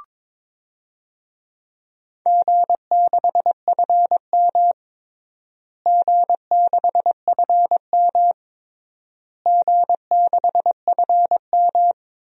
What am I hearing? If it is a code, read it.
Morse code, heard as a keyed tone, about 22 wpm, 700 Hz. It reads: G6FM G6FM G6FM
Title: DXCC-100 Call Signs - 4 Characters - Repeated 3x - 22wpm